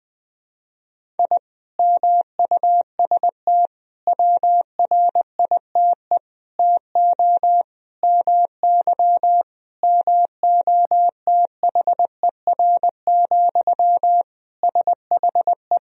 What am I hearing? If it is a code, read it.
Morse code, heard as a keyed tone, about 20 wpm, 700 Hz. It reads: I MUST WRITE TO MY MOTHER, SHE
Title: Sentences from Top 1000 Words - Review Part 1 - 20wpm